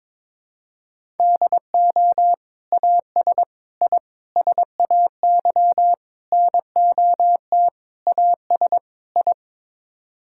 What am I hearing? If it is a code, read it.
Morse code, heard as a keyed tone, about 22 wpm, 700 Hz. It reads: DO AS I SAY NOT AS I
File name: practice / 22wpm / Sets of 7 Words - 22wpm